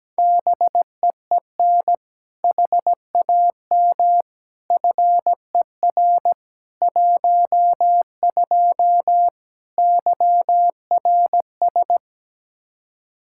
Morse code, heard as a keyed tone, about 17 wpm, 700 Hz. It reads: BEEN HAM FER 12 YRS
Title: Practice QSO - Exchange 3 - More Info - 17wpm